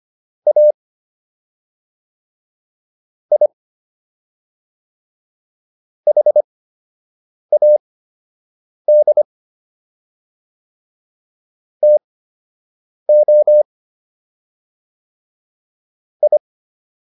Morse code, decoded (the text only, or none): A I HAD TO I